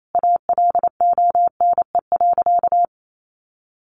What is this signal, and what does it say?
Morse code, heard as a keyed tone, about 28 wpm, 700 Hz. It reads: ALONE.
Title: Sentences from Top 1000 Words 28wpm